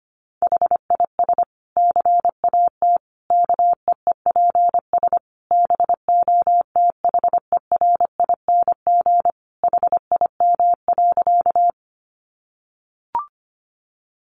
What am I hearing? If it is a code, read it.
Morse code, heard as a keyed tone, about 25 wpm, 700 Hz. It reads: HIS CAT KEEPS BOTHERING HIM.
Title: Additional Easy Phrases 25wpm